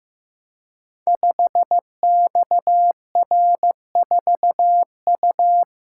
Morse code, heard as a keyed tone, about 15 wpm, 700 Hz. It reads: 5XR4U
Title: Code Groups - 5 Letter:Numbers - 15wpm